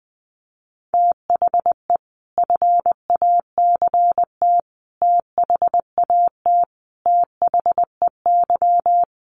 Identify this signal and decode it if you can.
Morse code, heard as a keyed tone, about 20 wpm, 700 Hz. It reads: THE FACT THAT THEY
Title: Sets of 5 Words Encore2 20wpm